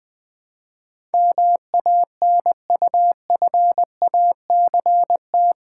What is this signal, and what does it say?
Morse code, heard as a keyed tone, about 20 wpm, 700 Hz. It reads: MANUFACT
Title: Build A Word - 4 Letters At A Time - 11-Letter Words - 20wpm